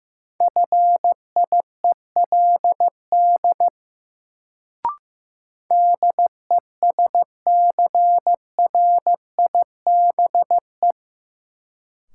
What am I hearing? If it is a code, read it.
Morse code, heard as a keyed tone, about 15 wpm, 700 Hz. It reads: FIELD DESCRIBE
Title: Sentences from Top 1000 Words - New Word Drill - 15wpm